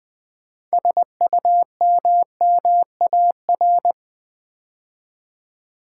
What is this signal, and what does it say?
Morse code, heard as a keyed tone, about 20 wpm, 700 Hz. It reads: SUMMAR
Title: Build A Word - 3 Letters At A Time - 9-Letter Words - 20wpm